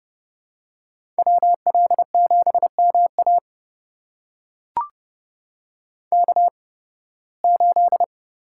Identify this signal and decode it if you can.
Morse code, heard as a keyed tone, about 31 wpm, spaced as for 20 wpm, 700 Hz. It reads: WL7MA K8
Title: USA Call Signs - 5 Characters - 8x Character Spacing - 30wpm